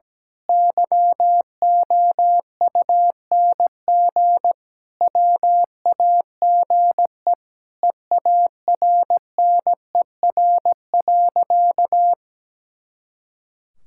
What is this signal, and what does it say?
Morse code, heard as a keyed tone, about 17 wpm, 700 Hz. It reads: YOUNG WAGE EARNER.
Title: Pangram Challenge v2 - 17wpm